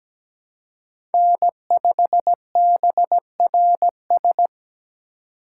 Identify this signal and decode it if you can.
Morse code, heard as a keyed tone, about 17 wpm, 700 Hz. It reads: N5BRS
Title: USA Call Signs - 5 Characters - 2x Character Spacing - 17wpm